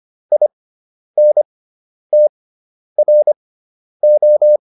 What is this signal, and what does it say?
Morse code, heard as a keyed tone, about 25 wpm, spaced as for 10 wpm, 600 Hz. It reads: INTRO